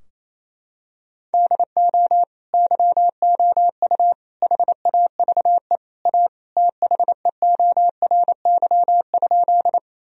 Morse code, heard as a keyed tone, about 28 wpm, 700 Hz. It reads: DO YOU HAVE A THEORY?
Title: Sentences from Top 1000 Words - Review Part 3 - 28wpm